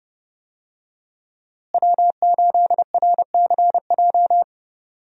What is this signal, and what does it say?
Morse code, heard as a keyed tone, about 30 wpm, 700 Hz. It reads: W8RCJ
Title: USA Call Signs - 5 Characters - Repeated 3x - 30wpm